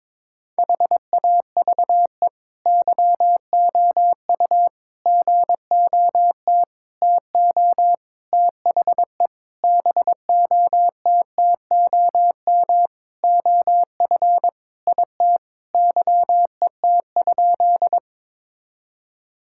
Morse code, heard as a keyed tone, about 22 wpm, 700 Hz. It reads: HAVE YOU GOT TO THE BOTTOM OF IT YET?
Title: Sentences from Top 1000 Words - Review Part 6 - 22wpm